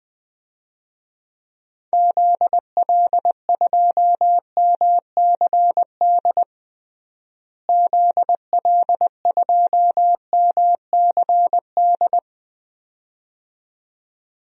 Morse code, heard as a keyed tone, about 21 wpm, 700 Hz. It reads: ZL2MCD ZL2MCD
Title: DXCC-100 Call Signs - 6 Characters - Repeated 2x - 20wpm